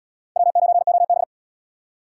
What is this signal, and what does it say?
Morse code, heard as a keyed tone, about 64 wpm, 700 Hz. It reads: W9OB